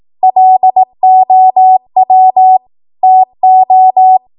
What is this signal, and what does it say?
Morse code, heard as a keyed tone, about 18 wpm, 750 Hz. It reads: LOW TO
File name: Morse / 18WPM